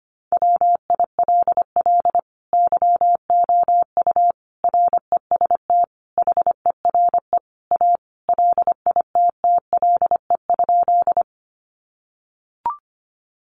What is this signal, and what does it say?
Morse code, heard as a keyed tone, about 25 wpm, 700 Hz. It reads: WILL YOU REST HERE A LITTLE?
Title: Sentences from Top 500 Words 25wpm